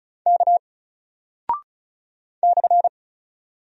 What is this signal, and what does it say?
Morse code, heard as a keyed tone, about 36 wpm, 700 Hz. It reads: K /
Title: Single Character (Letter:Number:Punctuation:Prosign) Alt 35wpm